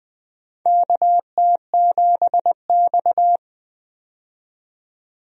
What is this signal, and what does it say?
Morse code, heard as a keyed tone, about 20 wpm, 700 Hz. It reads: KT7X
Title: USA Call Signs - 4 Characters - 20wpm